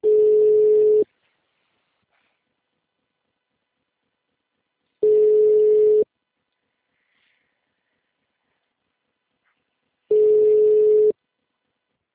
Ringback tone (Europe, 425 Hz).